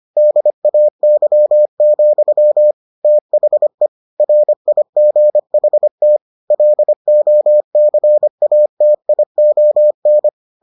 Morse code, heard as a keyed tone, about 25 wpm, 600 Hz. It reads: DAY, THE RIGHT LOCATION